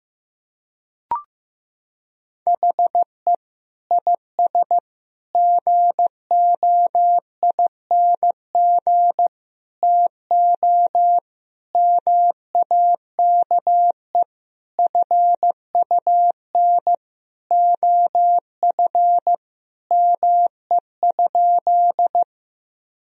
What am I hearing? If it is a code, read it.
Morse code, heard as a keyed tone, about 15 wpm, 700 Hz. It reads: HE IS GOING TO MAKE FUN OF ME?
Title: Sentences from Top 1000 Words - Review Part 4 - 15wpm